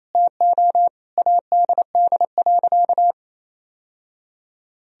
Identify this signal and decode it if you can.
Morse code, heard as a keyed tone, about 28 wpm, 700 Hz. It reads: TO ADD.
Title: Sentences from Top 1000 Words - Review Part 1 - 28wpm